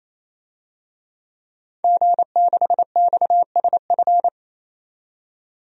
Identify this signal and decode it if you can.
Morse code, heard as a keyed tone, about 27 wpm, 700 Hz. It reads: G6XSF
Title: DXCC-100 Call Signs - 5 Characters - 28wpm